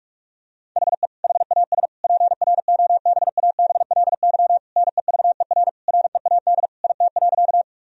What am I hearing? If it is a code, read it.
Morse code, heard as a keyed tone, about 47 wpm, 700 Hz. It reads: HE HAS PROBABLY NEVER READ IT.
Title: Sentences from Top 1000 Words - Review Part 4 - 45wpm